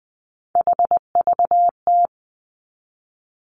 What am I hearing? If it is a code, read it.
Morse code, heard as a keyed tone, about 20 wpm, 700 Hz. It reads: HVT
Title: Three Challenging Letter-Number Combinations 20wpm